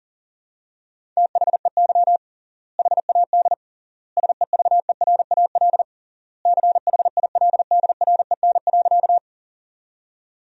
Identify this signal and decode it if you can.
Morse code, heard as a keyed tone, about 40 wpm, 700 Hz. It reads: THEY HAD SEVERAL CHILDREN.
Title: 3x Word-Spacing - Sentences from Top 400 Words - 40wpm